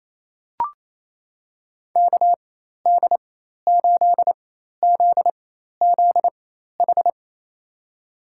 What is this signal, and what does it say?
Morse code, heard as a keyed tone, about 27 wpm, spaced as for 13 wpm, 700 Hz. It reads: KD8ZZH
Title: USA Call Signs - 6 Characters - 4x Character Spacing - 28wpm